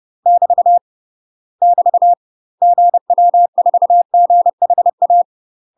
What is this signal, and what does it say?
Morse code, heard as a keyed tone, about 30 wpm, 700 Hz. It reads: <BT> <BT> GW4GHA
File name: UK MASTER QSOs TRAINING / 30WPM / UK MASTER QSOs 0043